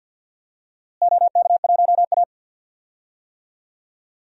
Morse code, heard as a keyed tone, about 51 wpm, 700 Hz. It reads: OK1A
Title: DXCC-100 Call Signs - 4 Characters - Repeated 2x - 50wpm